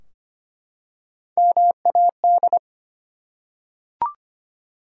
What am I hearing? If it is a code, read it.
Morse code, heard as a keyed tone, about 25 wpm, 700 Hz. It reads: MAD
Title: Three Letter Words - 25wpm